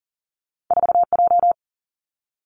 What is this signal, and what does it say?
Morse code, heard as a keyed tone, about 42 wpm, 700 Hz. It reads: 4J
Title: Two Challenging Letter-Number Combinations 40wpm